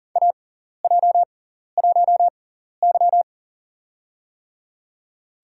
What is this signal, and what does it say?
Morse code, heard as a keyed tone, about 39 wpm, spaced as for 14 wpm, 700 Hz. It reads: AJ1Y
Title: USA Call Signs - 4 Characters - 6x Character Spacing - 40wpm